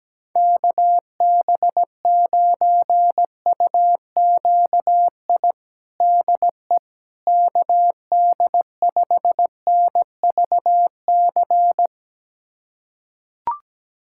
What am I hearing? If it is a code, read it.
Morse code, heard as a keyed tone, about 17 wpm, 700 Hz. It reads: KB9UQI DE KD5NVC
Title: USA Callsign DE Callsign - 6 Characters - 17wpm